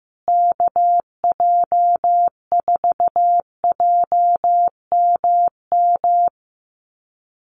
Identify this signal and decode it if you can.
Morse code, heard as a keyed tone, about 15 wpm, 700 Hz. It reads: KJ4JMM
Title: USA Call Signs Repeated 3x 15wpm